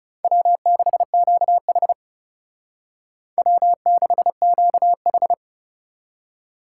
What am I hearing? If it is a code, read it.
Morse code, heard as a keyed tone, about 38 wpm, 700 Hz. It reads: W6QH W6QH